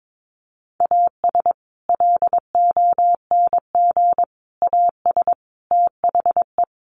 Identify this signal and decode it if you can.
Morse code, heard as a keyed tone, about 22 wpm, 700 Hz. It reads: AS LONG AS THE
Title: Sets of 4 Words 22wpm